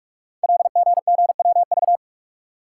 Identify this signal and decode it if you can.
Morse code, heard as a keyed tone, about 44 wpm, 700 Hz. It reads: LGGWV
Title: Code Groups - 5 Letters - 45wpm